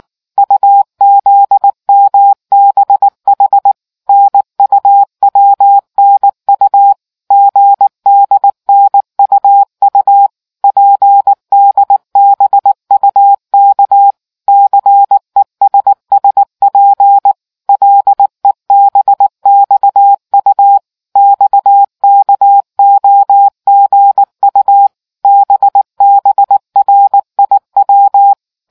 Morse code, heard as a keyed tone, about 19 wpm, 800 Hz. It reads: UZMBH NUWNU GDNUU PDBUK CESSP LEBXU XKOGU BBRIW